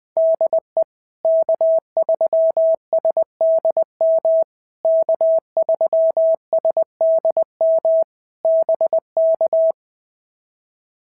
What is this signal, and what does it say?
Morse code, heard as a keyed tone, about 20 wpm, 650 Hz. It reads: DE K3SDM K3SDM BK
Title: POTA QSO Practice - 20wpm